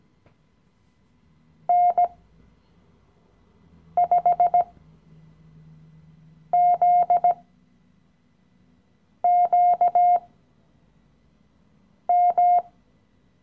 Morse code, heard as a keyed tone, about 17 wpm, spaced as for 4 wpm, 710 Hz. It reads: N5ZQM
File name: CW-Online-Kurs / Anfänger / Lektion 015